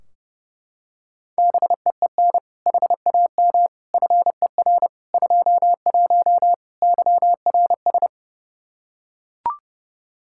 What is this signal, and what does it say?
Morse code, heard as a keyed tone, about 32 wpm, 700 Hz. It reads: BEEN HAM FER 21 YRS
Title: Practice QSO - Exchange 3 - More Info - 30wpm